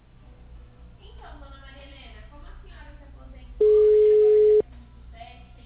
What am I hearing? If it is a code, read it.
Ringback tone (Europe, 425 Hz), with speech.